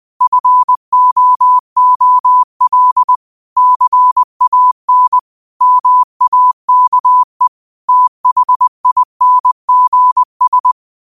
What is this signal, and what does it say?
Morse code, heard as a keyed tone, about 20 wpm, 1000 Hz. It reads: FOOL CAN MAKE THINGS
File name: qotdfiles / 20wpm